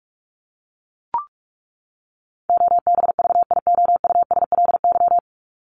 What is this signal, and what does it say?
Morse code, heard as a keyed tone, about 44 wpm, 700 Hz. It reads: OBVIOUSLY
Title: LY Suffix - Top 100 Words with Base Words 45wpm